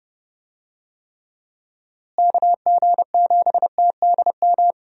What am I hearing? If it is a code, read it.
Morse code, heard as a keyed tone, about 30 wpm, 700 Hz. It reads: KG7TDM
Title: USA Call Signs - 6 Characters - Repeated 2x - 30wpm